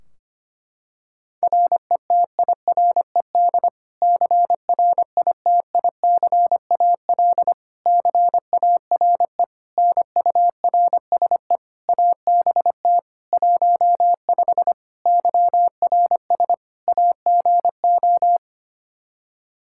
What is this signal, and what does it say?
Morse code, heard as a keyed tone, about 25 wpm, 700 Hz. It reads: RETIRED CRITICAL CARE NURSE ABT 15 YRS AGO
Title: Practice QSO - Exchange 3 - More Info - 25wpm